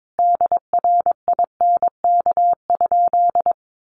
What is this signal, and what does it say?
Morse code, heard as a keyed tone, about 22 wpm, 700 Hz. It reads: DRINK?
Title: Sentences from Top 1500 Words 22wpm